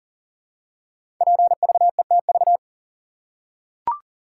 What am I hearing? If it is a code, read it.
Morse code, heard as a keyed tone, about 40 wpm, 700 Hz. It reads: PVETV
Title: Code Groups - 5 Letters - 40wpm